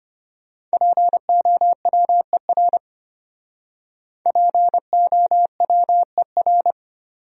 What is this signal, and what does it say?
Morse code, heard as a keyed tone, about 33 wpm, 700 Hz. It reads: POWER POWER